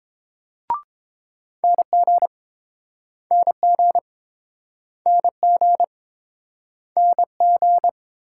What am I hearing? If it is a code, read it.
Morse code, heard as a keyed tone, about 29 wpm, 700 Hz. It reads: NG NG NG NG